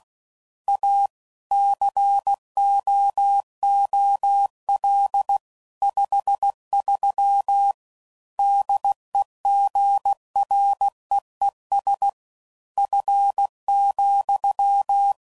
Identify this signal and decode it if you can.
Morse code, heard as a keyed tone, about 16 wpm, 800 Hz. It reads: A COOL 53 DEGREES F,
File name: CW Code Groups @15WPM / Weather Sets / Weather 5 1